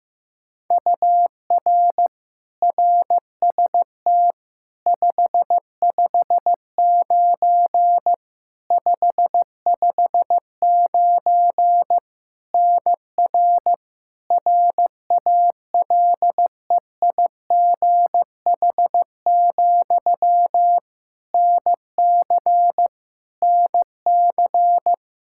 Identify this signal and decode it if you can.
Morse code, heard as a keyed tone, about 16 wpm, 700 Hz. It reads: UR RST 559 559 NR RALEIGH, NC NC
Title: K1USN SST QSO Practice 15wpm